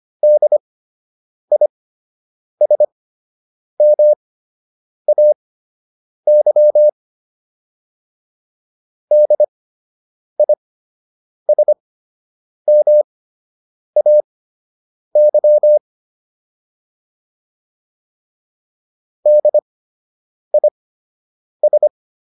Morse code, heard as a keyed tone, about 25 wpm, spaced as for 8 wpm, 600 Hz. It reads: DISMAY DISMAY DIS
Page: fundamental-practice-files